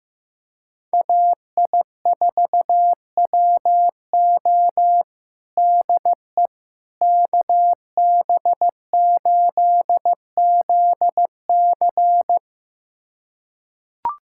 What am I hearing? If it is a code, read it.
Morse code, heard as a keyed tone, about 15 wpm, 700 Hz. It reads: AI4WO DE KB8ZC
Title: USA Callsign DE Callsign - 5 Characters - 15wpm